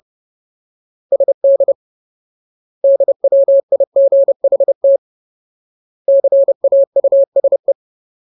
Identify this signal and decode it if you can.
Morse code, heard as a keyed tone, about 30 wpm, 550 Hz. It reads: SD DWIGHT CAUSE
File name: Ultimate Word List / 30WPM Ultimate